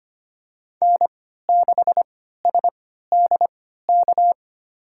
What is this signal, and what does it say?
Morse code, heard as a keyed tone, about 25 wpm, spaced as for 14 wpm, 700 Hz. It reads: N6SDK
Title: USA Call Signs - 5 Characters - 3x Character Spacing - 25wpm